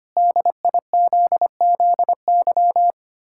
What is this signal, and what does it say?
Morse code, heard as a keyed tone, about 25 wpm, 700 Hz. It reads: DIZZY
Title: Pangram Challenge v2 - 25wpm